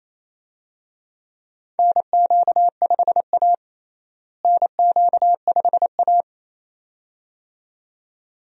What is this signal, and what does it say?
Morse code, heard as a keyed tone, about 28 wpm, 700 Hz. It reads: NQ5A NQ5A
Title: USA Call Signs - 4 Characters - Repeated 2x - 28wpm